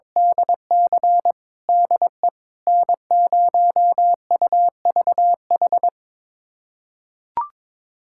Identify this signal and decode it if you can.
Morse code, heard as a keyed tone, about 21 wpm, 700 Hz. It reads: DC DE N0UVH
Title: USA Callsign DE Callsign - 5 Characters - 22wpm